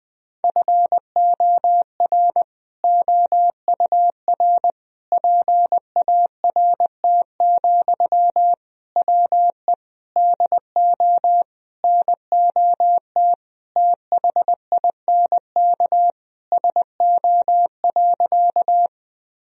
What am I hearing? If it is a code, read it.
Morse code, heard as a keyed tone, about 20 wpm, 700 Hz. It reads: FOR OUR PART, WE DO NOT THINK SO.
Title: Sentences from Top 1000 Words - Review Part 1 - 20wpm